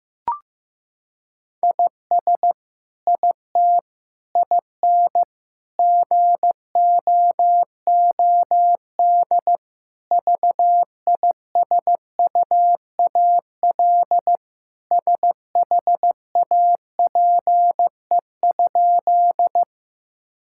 Morse code, heard as a keyed tone, about 15 wpm, 700 Hz. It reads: IS IT IN GOOD VISUAL SHAPE?
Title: Sentences from Top 1000 Words - Review Part 6 - 15wpm